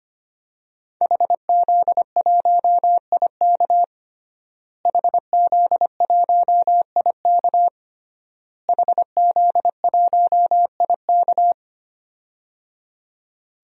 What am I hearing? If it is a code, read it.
Morse code, heard as a keyed tone, about 25 wpm, 700 Hz. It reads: HZ1IK HZ1IK HZ1IK
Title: DXCC-100 Call Signs - 5 Characters - Repeated 3x - 25wpm